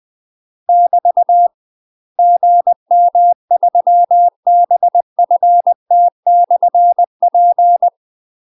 Morse code, heard as a keyed tone, about 20 wpm, 700 Hz. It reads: <BT> GM3BFT/P
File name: UK MASTER QSOs TRAINING / 20WPM / UK MASTER QSOs 0059